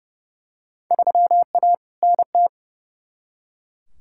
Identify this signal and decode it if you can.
Morse code, heard as a keyed tone, about 30 wpm, 700 Hz. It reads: 3A NT
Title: Field Day Exchanges Encore - 30wpm